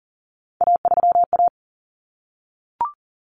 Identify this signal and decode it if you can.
Morse code, heard as a keyed tone, about 40 wpm, 700 Hz. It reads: A3A
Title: Common Cut Numbers - 3-Digit 40wpm